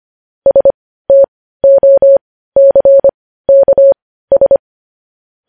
Morse code, heard as a keyed tone, about 25 wpm, spaced as for 15 wpm, 550 Hz. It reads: STOCKS
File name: podcast / News headlines-1x-25x15